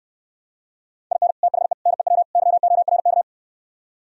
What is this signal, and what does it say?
Morse code, heard as a keyed tone, about 68 wpm, 700 Hz. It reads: IN THE NEW YORK